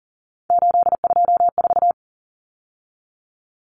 Morse code, heard as a keyed tone, about 40 wpm, 700 Hz. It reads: 824
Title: Numbers - 3-Digit 40wpm